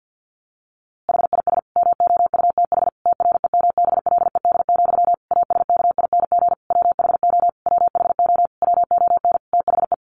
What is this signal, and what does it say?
Morse code, heard as a keyed tone, about 50 wpm, 700 Hz. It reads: HIS MOUTH TREMBLED, ASKING WHO WHO WON THE
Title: A Sound Of Thunder 50wpm